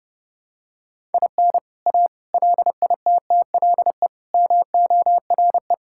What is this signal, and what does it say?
Morse code, heard as a keyed tone, about 32 wpm, 700 Hz. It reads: IN A LITTLE MORE
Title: Sentences from Top 1000 Words - Review Part 5 - 30wpm